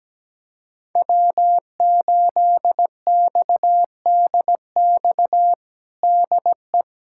Morse code, heard as a keyed tone, about 17 wpm, 700 Hz. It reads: W8XDX DE
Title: USA Callsign DE Callsign - 5 Characters - 17wpm